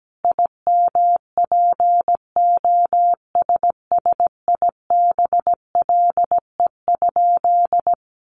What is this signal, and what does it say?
Morse code, heard as a keyed tone, about 17 wpm, 700 Hz. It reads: IMPOSSIBLE?
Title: Sentences from Top 1100 Words 17wpm